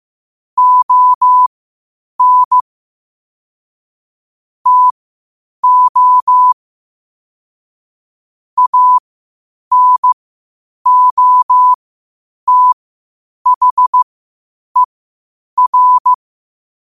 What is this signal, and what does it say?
Morse code, heard as a keyed tone, about 15 wpm, spaced as for 8 wpm, 1000 Hz. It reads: ON TO ANOTHER
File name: qotdfiles / 8wpm